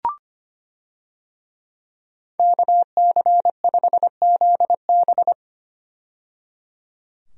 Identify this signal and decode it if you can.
Morse code, heard as a keyed tone, about 25 wpm, 700 Hz. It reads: KC5ZB
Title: USA Call Signs - 5 Characters - 25wpm